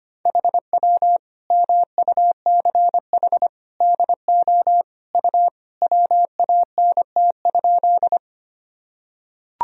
Morse code, heard as a keyed tone, about 25 wpm, 700 Hz. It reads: HW MUCH DO U WANT?
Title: Sentences from Top 200 Words and Abbreviations 25wpm